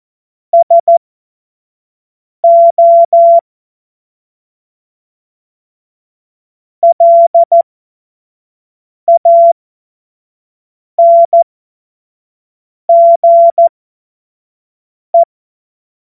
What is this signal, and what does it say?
Morse code, heard as a keyed tone, about 14 wpm, spaced as for 5 wpm, 680 Hz. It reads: SO LANGE
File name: Witz-29-Juli-2025-25-F70